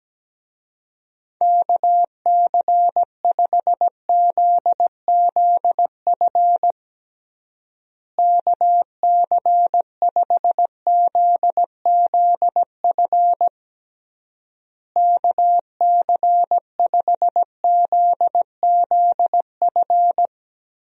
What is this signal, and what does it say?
Morse code, heard as a keyed tone, about 17 wpm, 700 Hz. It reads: KC5ZZF KC5ZZF KC5ZZF
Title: USA Call Signs - 6 Characters - Repeated 3x - 17wpm